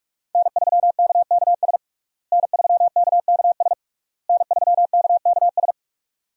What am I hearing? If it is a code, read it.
Morse code, heard as a keyed tone, about 41 wpm, 700 Hz. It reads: N3KKS N3KKS N3KKS
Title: USA Call Signs - 5 Characters - Repeated 3x - 45wpm